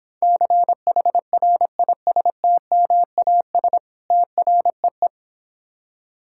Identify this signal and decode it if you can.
Morse code, heard as a keyed tone, about 26 wpm, 700 Hz. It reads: CHRISTMAS TREE